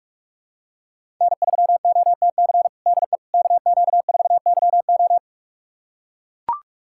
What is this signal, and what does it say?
Morse code, heard as a keyed tone, about 41 wpm, 700 Hz. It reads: N3OTC DE KQ4YO
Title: USA Callsign DE Callsign - 5 Characters - 45wpm